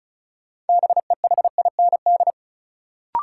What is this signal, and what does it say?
Morse code, heard as a keyed tone, about 36 wpm, 700 Hz. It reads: BEHIND
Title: Sentences from Top 500 Words - New Word Drill - 35wpm